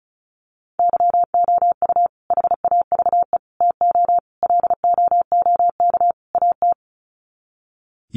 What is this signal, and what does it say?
Morse code, heard as a keyed tone, about 34 wpm, 700 Hz. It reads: YOU HAVE TO LOOK AT
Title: Sets of 5 Words Encore 35wpm